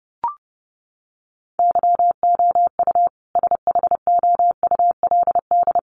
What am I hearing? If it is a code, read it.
Morse code, heard as a keyed tone, about 30 wpm, 700 Hz. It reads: YOU SHOULD
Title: Sentences from Top 400 Words 30wpm